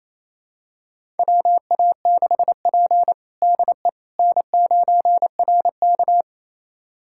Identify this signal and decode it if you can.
Morse code, heard as a keyed tone, about 27 wpm, 700 Hz. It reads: WA6P DE N9RK
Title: USA Callsign DE Callsign - 4 Characters - 28wpm